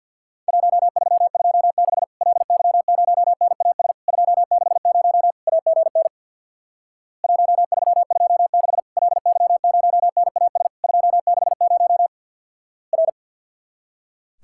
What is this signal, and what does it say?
Morse code, heard as a keyed tone, about 50 wpm, 700 Hz. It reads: 1326 LY0NAS 260 AGN 1326 LY0NAS 260 R
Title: Worked All Europe DX Contest - QTC Exchange with Repeats - 50wpm